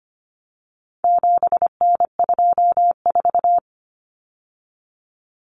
Morse code, heard as a keyed tone, about 25 wpm, 700 Hz. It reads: 7N24
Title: Common Cut Numbers - 4-Digit 25wpm